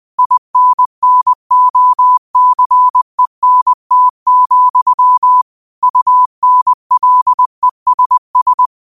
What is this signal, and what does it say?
Morse code, heard as a keyed tone, about 20 wpm, 1000 Hz. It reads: INNOCENT, UNLESS
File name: 20wpm